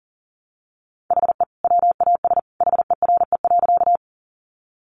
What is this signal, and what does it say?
Morse code, heard as a keyed tone, about 37 wpm, 700 Hz. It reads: HE WAS HERE.
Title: Sentences from Top 200 Words 40wpm